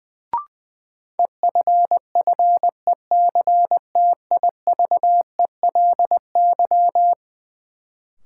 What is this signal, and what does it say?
Morse code, heard as a keyed tone, about 20 wpm, 700 Hz. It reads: EFFECTIVELY
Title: Sentences from Top 1400 Words - New Word Drill - 20wpm